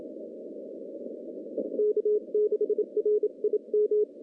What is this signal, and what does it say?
Morse code, heard as a keyed tone, about 32 wpm, spaced as for 27 wpm, 400 Hz. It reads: K6RIM
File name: contests / CW-SS-k5tr- -0200